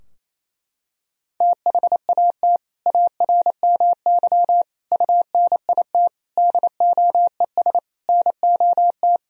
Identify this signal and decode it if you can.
Morse code, heard as a keyed tone, about 28 wpm, 700 Hz. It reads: THAT ARMY UNIT DOES NOT
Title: Sentences from Top 1000 Words - Review Part 2 - 28wpm